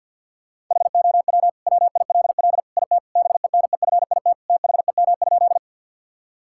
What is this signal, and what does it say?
Morse code, heard as a keyed tone, about 50 wpm, 700 Hz. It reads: HOW WILL IT BENEFIT THEM?
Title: Sentences from Top 1000 Words - Review Part 6 - 50wpm